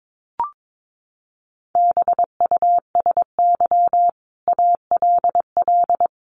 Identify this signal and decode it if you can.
Morse code, heard as a keyed tone, about 22 wpm, 700 Hz. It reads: BUSY ALL
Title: Sentences from Top 1500 Words 22wpm